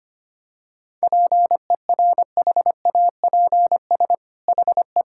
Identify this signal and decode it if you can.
Morse code, heard as a keyed tone, about 25 wpm, 700 Hz. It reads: PERHAPS HE
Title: Sentences from Top 1600 Words 25wpm